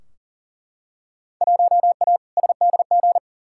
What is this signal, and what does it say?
Morse code, heard as a keyed tone, about 40 wpm, 700 Hz. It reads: 1A SDG
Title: Field Day Exchanges Encore - 40wpm